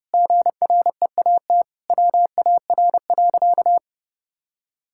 Morse code, heard as a keyed tone, about 30 wpm, 700 Hz. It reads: GREAT WAR.
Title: Sentences from Top 1600 Words 30wpm